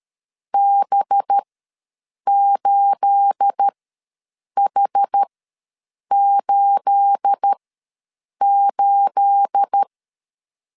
Morse code, heard as a keyed tone, about 13 wpm, spaced as for 7 wpm, 800 Hz. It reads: B8H88
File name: LEZIONI / 9 28 ws 12wpm 3354